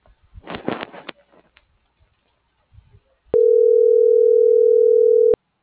Ringback tone (North America), 440 + 480 Hz.